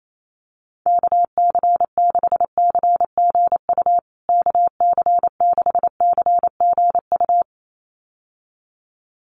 Morse code, heard as a keyed tone, about 28 wpm, 700 Hz. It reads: KC6CGU KC6CGU
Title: USA Call Signs Repeated 2x 28wpm